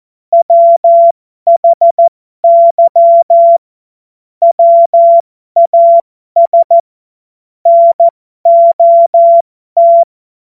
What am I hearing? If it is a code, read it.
Morse code, heard as a keyed tone, about 14 wpm, spaced as for 12 wpm, 680 Hz. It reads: WHY WAS NOT
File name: Morsekurs / Witze